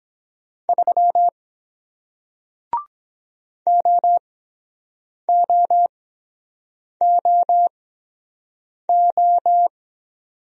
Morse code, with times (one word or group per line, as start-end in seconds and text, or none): 0.7-1.3 3
3.7-9.7 OOOO